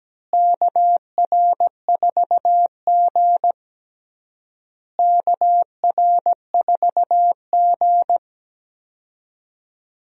Morse code, heard as a keyed tone, about 17 wpm, 700 Hz. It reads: KR4G KR4G
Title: USA Call Signs - 4 Characters - Repeated 3x - 17wpm